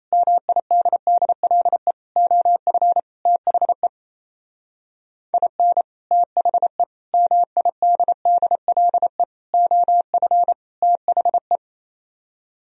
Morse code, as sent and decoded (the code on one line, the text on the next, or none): -- .. -.. -.. .-.. / --- ..-. / - .... / .. -. / - .... / -- .. -.. -.. .-.. / --- ..-. / - .... .
MIDDLE OF THE IN THE MIDDLE OF THE